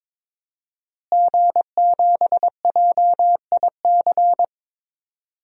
Morse code, heard as a keyed tone, about 21 wpm, 700 Hz. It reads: G7JIC
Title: DXCC-100 Call Signs - 5 Characters - Repeated 2x - 22wpm